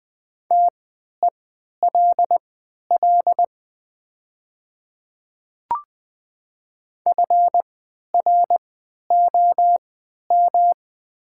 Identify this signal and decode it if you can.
Morse code, heard as a keyed tone, about 20 wpm, spaced as for 11 wpm, 690 Hz. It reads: TELL FROM
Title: 3x Character Spacing - Top 200 Words - 20wpm